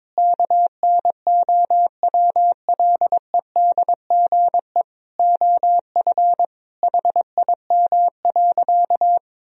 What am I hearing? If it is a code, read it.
Morse code, heard as a keyed tone, about 22 wpm, 700 Hz. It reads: KNOWLEDGE OF HIM.
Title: Sentences from Top 1000 Words - Review Part 3 - 22wpm